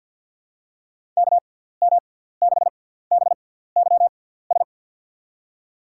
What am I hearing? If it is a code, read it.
Morse code, heard as a keyed tone, about 49 wpm, spaced as for 17 wpm, 700 Hz. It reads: KM6BYS
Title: USA Call Signs - 6 Characters - 6x Character Spacing - 50wpm